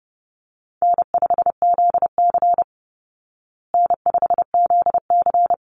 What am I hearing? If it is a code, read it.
Morse code, heard as a keyed tone, about 28 wpm, 700 Hz. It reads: N5ZC N5ZC
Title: Call Signs Repeated 3x - 30wpm